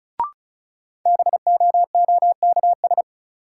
Morse code, heard as a keyed tone, about 36 wpm, 700 Hz. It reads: BOOKS
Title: Sentences from Top 1000 Words - Review Part 4 - New Word Drill - 35wpm